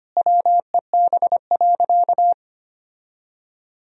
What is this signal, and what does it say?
Morse code, heard as a keyed tone, about 25 wpm, 700 Hz. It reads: WEB.
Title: Sentences from Top 1000 Words - Review Part 4 - 25wpm